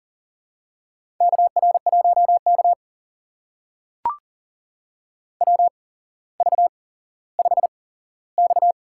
Morse code, heard as a keyed tone, about 41 wpm, spaced as for 17 wpm, 700 Hz. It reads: KR1K WV5X
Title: USA Call Signs - 4 Characters - 8x Character Spacing - 40wpm